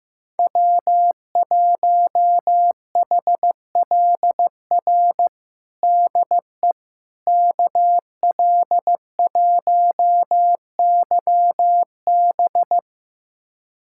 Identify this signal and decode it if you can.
Morse code, heard as a keyed tone, about 15 wpm, 700 Hz. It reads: W1HLR DE KL1YB
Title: USA Callsign DE Callsign - 5 Characters - 15wpm